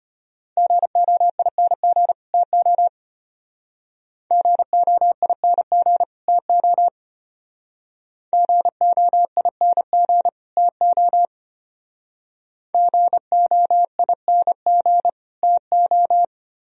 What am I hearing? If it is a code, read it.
Morse code, heard as a keyed tone, about 31 wpm, 700 Hz. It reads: GOING TO GOING TO GOING TO GOING TO
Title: Speed-Racing - Sets of 2 Words 25wpm